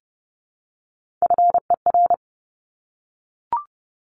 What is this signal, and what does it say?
Morse code, heard as a keyed tone, about 30 wpm, 700 Hz. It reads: FER
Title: Abbreviations Repeated 3x - 30wpm